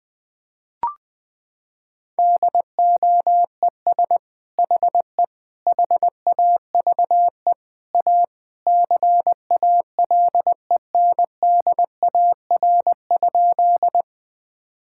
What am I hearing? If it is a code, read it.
Morse code, heard as a keyed tone, about 21 wpm, 700 Hz. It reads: DOES HE HAVE A CALENDAR?
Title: Sentences from Top 1600 Words 20wpm